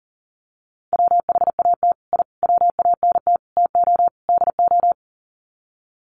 Morse code, heard as a keyed tone, about 40 wpm, 700 Hz. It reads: WHAT I WANT TO DO
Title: Sets of 5 Words Encore 40wpm